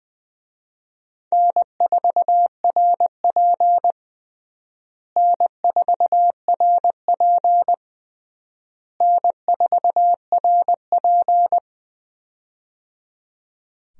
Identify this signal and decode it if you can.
Morse code, heard as a keyed tone, about 20 wpm, 700 Hz. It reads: N4RP N4RP N4RP
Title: USA Call Signs - 4 Characters - Repeated 3x - 20wpm